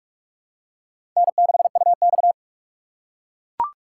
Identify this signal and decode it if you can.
Morse code, heard as a keyed tone, about 43 wpm, 700 Hz. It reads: N6UX
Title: USA Call Signs - 6x Character Spacing - 45wpm